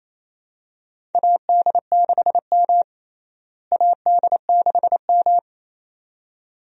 Morse code, heard as a keyed tone, about 28 wpm, 700 Hz. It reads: AD6M AD6M
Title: USA Call Signs - 4 Characters - Repeated 2x - 28wpm